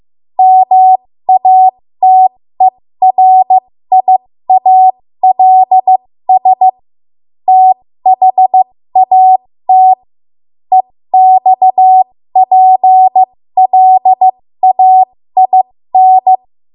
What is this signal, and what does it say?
Morse code, heard as a keyed tone, about 15 wpm, spaced as for 13 wpm, 750 Hz. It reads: MATERIALS THAT EXPLAIN
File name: Morse / 13WPM